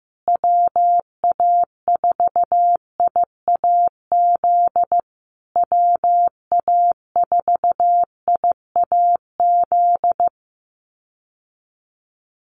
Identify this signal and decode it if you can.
Morse code, heard as a keyed tone, about 15 wpm, 700 Hz. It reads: WA4IAZ WA4IAZ
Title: USA Call Signs Repeated 3x 15wpm